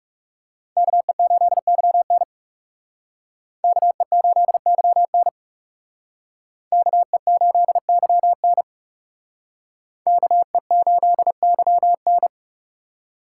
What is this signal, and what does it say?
Morse code, heard as a keyed tone, about 36 wpm, 700 Hz. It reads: KE8YN KE8YN KE8YN KE8YN
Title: Speed Racing - USA 2x2 Call Signs 30wpm